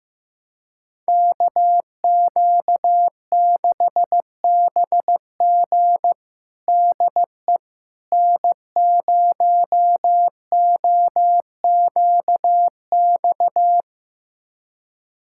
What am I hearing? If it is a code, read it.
Morse code, heard as a keyed tone, about 16 wpm, 700 Hz. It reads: KQ6BG DE N0OQX
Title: USA Callsign DE Callsign - 5 Characters - 15wpm